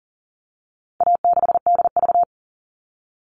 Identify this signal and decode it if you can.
Morse code, heard as a keyed tone, about 40 wpm, 700 Hz. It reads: A6DV
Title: All Cut Numbers - 4-Digit 40wpm